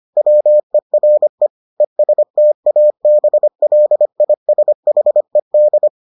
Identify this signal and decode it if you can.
Morse code, heard as a keyed tone, about 25 wpm, 600 Hz. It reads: WERE ESTABLISHED